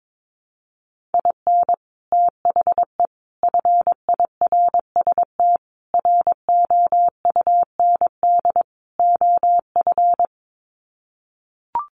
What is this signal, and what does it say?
Morse code, heard as a keyed tone, about 22 wpm, 700 Hz. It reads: IN THE FIRST ROUND OF
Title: Sets of 5 Words Encore 22wpm